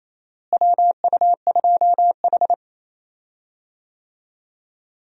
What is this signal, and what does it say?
Morse code, heard as a keyed tone, about 28 wpm, 700 Hz. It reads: WU2H
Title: USA Call Signs - 4 Characters - Repeated 3x - 28wpm